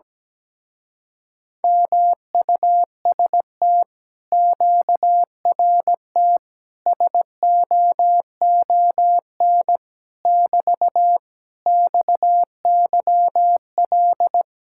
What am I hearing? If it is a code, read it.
Morse code, heard as a keyed tone, about 17 wpm, 700 Hz. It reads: MUST QRT SOON <BT> XYL
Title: Speed-Racing Common QSO Phrases Encore - 17wpm